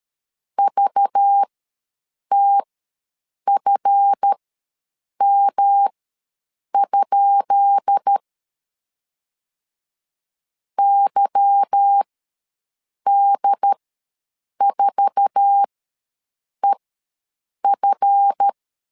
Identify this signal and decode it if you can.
Morse code, heard as a keyed tone, about 13 wpm, spaced as for 7 wpm, 800 Hz. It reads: VTFM? YD4EF
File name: LEZIONI / 10 SIGNS / 10B INT ws 12wpm 3527